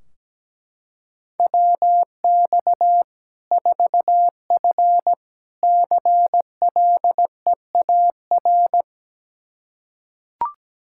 Morse code, heard as a keyed tone, about 17 wpm, 700 Hz. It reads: WX 4F CLEAR
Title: Practice QSO - Exchange 2 - Info - 17wpm